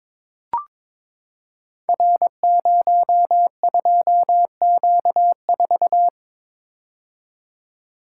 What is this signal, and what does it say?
Morse code, heard as a keyed tone, about 22 wpm, 690 Hz. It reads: R02Q4
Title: Code Groups - 5 Letter:Numbers - 22wpm